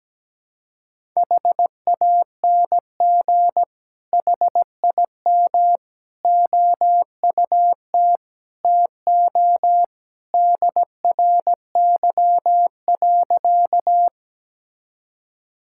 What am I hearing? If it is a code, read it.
Morse code, heard as a keyed tone, about 17 wpm, 700 Hz. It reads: HANG HIM OUT TO DRY.
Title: Sentences from Top 1600 Words 17wpm